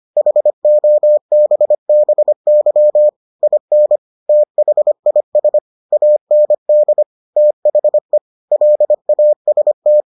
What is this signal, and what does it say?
Morse code, heard as a keyed tone, about 25 wpm, 600 Hz. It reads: HOBBY IN THIS AND THE LAST